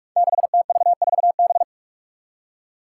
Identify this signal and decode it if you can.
Morse code, heard as a keyed tone, about 43 wpm, 700 Hz. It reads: 6TV4B
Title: Five Challenging Letter-Number Combinations 45wpm